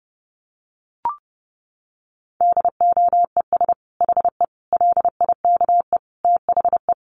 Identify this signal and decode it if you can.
Morse code, heard as a keyed tone, about 30 wpm, 700 Hz. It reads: DOES HE LIKE THE
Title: Sentences from Top 600 Words 30wpm